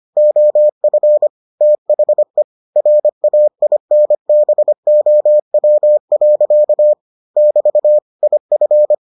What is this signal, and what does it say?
Morse code, heard as a keyed tone, about 25 wpm, 600 Hz. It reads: OF THE RAINBOW. <BT> IF